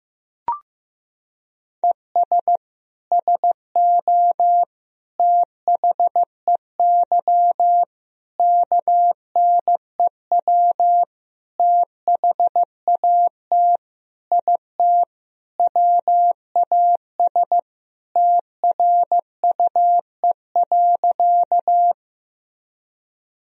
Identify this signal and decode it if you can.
Morse code, heard as a keyed tone, about 15 wpm, 700 Hz. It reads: ES SO THEY KNEW THAT IT WAS TRUE.
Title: Sentences from Top 400 Words and Abbreviations 15wpm